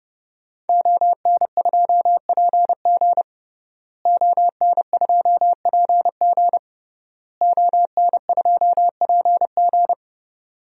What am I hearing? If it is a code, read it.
Morse code, heard as a keyed tone, about 30 wpm, 700 Hz. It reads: ON2PG ON2PG ON2PG
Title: DXCC-100 Call Signs - 5 Characters - Repeated 3x - 30wpm